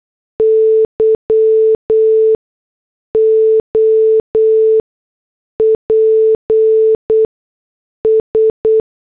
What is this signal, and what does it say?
Morse code, heard as a keyed tone, about 8 wpm, 440 Hz. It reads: Y O P S
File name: morse / headlines